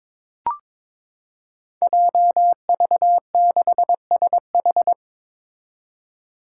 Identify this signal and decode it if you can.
Morse code, heard as a keyed tone, about 22 wpm, 700 Hz. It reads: JV6SH
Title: Five Challenging Letter-Number Combinations 22wpm